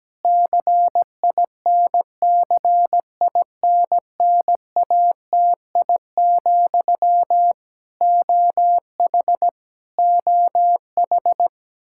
Morse code, heard as a keyed tone, about 17 wpm, 700 Hz. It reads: CINCINNATI, OH OH
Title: Practice QSO - Exchange 1 - Beginning - 17wpm